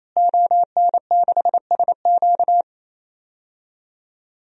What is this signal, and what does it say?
Morse code, heard as a keyed tone, about 27 wpm, 700 Hz. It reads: ON6SQ
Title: DXCC-100 Call Signs - 5 Characters - Repeated 3x - 28wpm